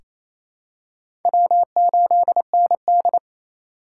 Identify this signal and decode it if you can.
Morse code, heard as a keyed tone, about 28 wpm, 700 Hz. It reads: W8ND
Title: USA Call Signs - 4 Characters - 3x Character Spacing - 28wpm